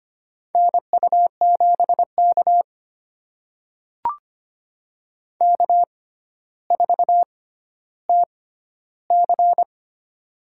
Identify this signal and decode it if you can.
Morse code, heard as a keyed tone, about 25 wpm, spaced as for 13 wpm, 700 Hz. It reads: NU7K K4TC
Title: USA Call Signs - 4 Characters - 6x Character Spacing - 25wpm